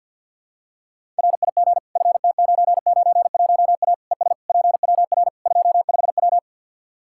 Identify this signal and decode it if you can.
Morse code, heard as a keyed tone, about 52 wpm, 700 Hz. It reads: RIG FT991A ES PWR 25W
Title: Practice QSO - Exchange 2 - Info - 50wpm